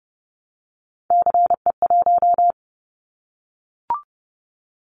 Morse code, heard as a keyed tone, about 30 wpm, 700 Hz. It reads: CE1
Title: Callsign Prefixes for Top 100 DXCC - 30wpm